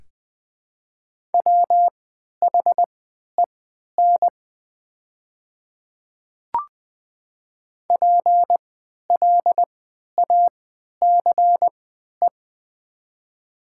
Morse code, heard as a keyed tone, about 20 wpm, spaced as for 11 wpm, 700 Hz. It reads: WHEN PLACE
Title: 3x Character Spacing - Top 300 Words - 20wpm